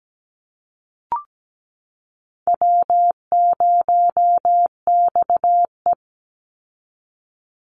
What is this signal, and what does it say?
Morse code, heard as a keyed tone, about 17 wpm, 700 Hz. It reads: W0XE
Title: Call Signs - 17wpm